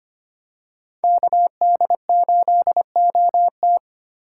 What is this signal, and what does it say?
Morse code, heard as a keyed tone, about 25 wpm, 700 Hz. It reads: KD8OT
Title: USA Call Signs - 2x Character Spacing - 25wpm